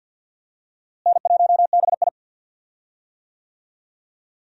Morse code, heard as a keyed tone, about 50 wpm, 700 Hz. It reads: N1BI
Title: USA Call Signs - 4 Characters - Repeated 2x - 50wpm